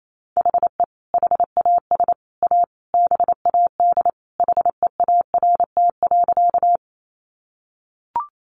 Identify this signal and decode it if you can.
Morse code, heard as a keyed tone, about 28 wpm, 700 Hz. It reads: HE HAS A BAD HEART.
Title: Sentences from Top 1200 Words 28wpm